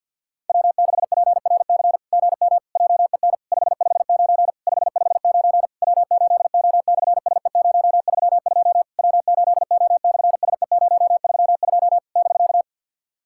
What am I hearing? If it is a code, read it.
Morse code, heard as a keyed tone, about 48 wpm, 690 Hz. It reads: W6PRC GM JEN 559 559 W8O/SE032 W8O/SE032 <BK>
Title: SOTA QSO Practice 50wpm